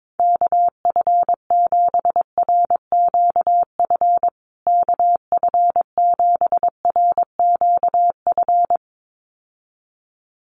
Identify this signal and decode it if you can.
Morse code, heard as a keyed tone, about 22 wpm, 700 Hz. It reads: KF7RQF KF7RQF
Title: USA Call Signs Repeated 2x 22wpm